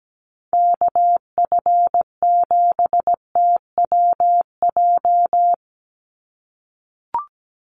Morse code, heard as a keyed tone, about 17 wpm, 700 Hz. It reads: KF7TWJ
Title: USA Call Signs 17wpm